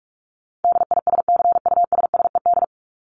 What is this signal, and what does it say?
Morse code, heard as a keyed tone, about 44 wpm, 700 Hz. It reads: DISCUSSED
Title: ED Suffix - Top 100 Words with Base Words 45wpm